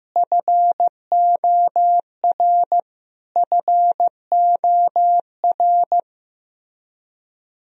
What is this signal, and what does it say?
Morse code, heard as a keyed tone, about 16 wpm, 700 Hz. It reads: FOR FOR
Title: Top 100 Words Repeated 15wpm